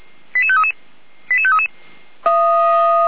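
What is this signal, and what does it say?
Five-tone selective call (ZVEI), heard as quick runs of five tones, tones of about 70 ms. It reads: 88533 88533
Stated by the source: ZVEI five-tone selective call received via SDR.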